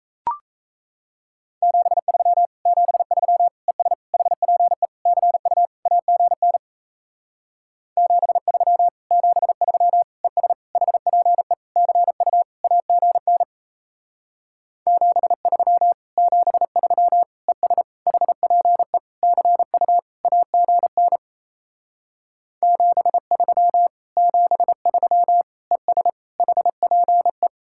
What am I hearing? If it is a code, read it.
Morse code, heard as a keyed tone, about 37 wpm, 700 Hz. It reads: 73 73 ES HPE CU AGN 73 73 ES HPE CU AGN 73 73 ES HPE CU AGN 73 73 ES HPE